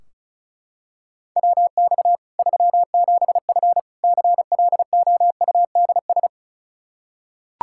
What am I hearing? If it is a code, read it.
Morse code, heard as a keyed tone, about 36 wpm, 700 Hz. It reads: WX 37F CLOUDS
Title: Practice QSO - Exchange 2 - Info - 35wpm